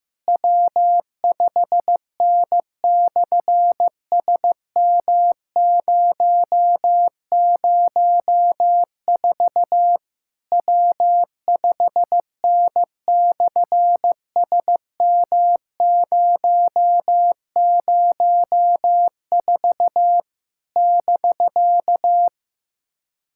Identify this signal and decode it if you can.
Morse code, heard as a keyed tone, about 15 wpm, 700 Hz. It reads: W5N/SM004 W5N/SM004 <BK>
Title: SOTA QSO Practice 15wpm